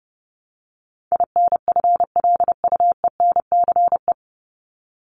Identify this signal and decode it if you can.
Morse code, heard as a keyed tone, about 30 wpm, 700 Hz. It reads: INFLUENCE
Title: Top 1001-1250 Words 30wpm